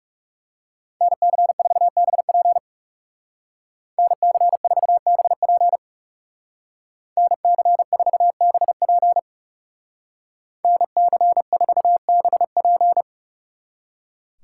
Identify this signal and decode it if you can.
Morse code, heard as a keyed tone, about 39 wpm, 700 Hz. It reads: NC4BP NC4BP NC4BP NC4BP